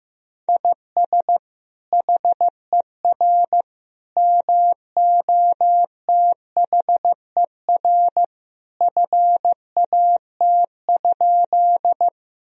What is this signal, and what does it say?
Morse code, heard as a keyed tone, about 16 wpm, 700 Hz. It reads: IS HER MOTHER FAT?
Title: Sentences from Top 1000 Words - Review Part 4 - 15wpm